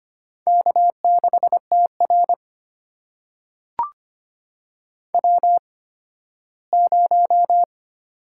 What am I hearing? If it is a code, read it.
Morse code, heard as a keyed tone, about 26 wpm, spaced as for 15 wpm, 700 Hz. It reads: K6TR W0
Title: USA Call Signs - 4 Characters - 8x Character Spacing - 25wpm